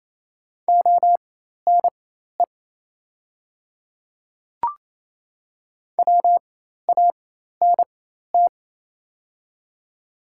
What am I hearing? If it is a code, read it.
Morse code, heard as a keyed tone, about 28 wpm, spaced as for 13 wpm, 700 Hz. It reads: ONE WANT
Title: 4x Character Spacing - Top 200 Words - 28wpm